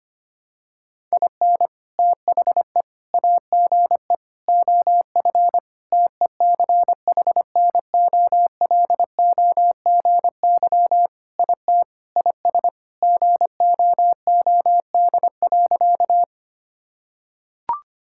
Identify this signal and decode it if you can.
Morse code, heard as a keyed tone, about 25 wpm, 700 Hz. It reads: IN THE AGE OF TECHNOLOGY IT IS GOOD.
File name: practice / 25wpm / Sentences with 3-Word Phrases - 25wpm